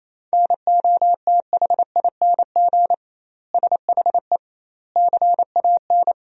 Morse code, heard as a keyed tone, about 28 wpm, 700 Hz. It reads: NOTHING SHE CAN
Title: 2x Word-Spacing - Sentences from Top 500 Words - 28wpm